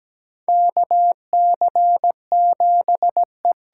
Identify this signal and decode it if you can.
Morse code, heard as a keyed tone, about 17 wpm, 700 Hz. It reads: KC7E
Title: USA Call Signs - 6 Characters - 8x Character Spacing - 17wpm